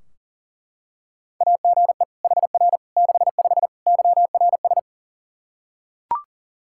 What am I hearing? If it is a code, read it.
Morse code, heard as a keyed tone, about 40 wpm, 700 Hz. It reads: AGE HR 65 YRS
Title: Practice QSO - Exchange 3 - More Info - 40wpm